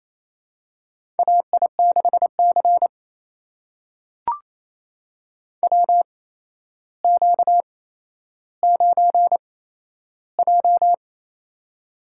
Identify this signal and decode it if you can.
Morse code, heard as a keyed tone, about 28 wpm, spaced as for 12 wpm, 700 Hz. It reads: AI6C WQ9J
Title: USA Call Signs - 4 Characters - 8x Character Spacing - 28wpm